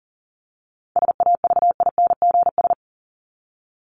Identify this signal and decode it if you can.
Morse code, heard as a keyed tone, about 40 wpm, 700 Hz. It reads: SAVINGS
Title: Top 1001-1250 Words 40wpm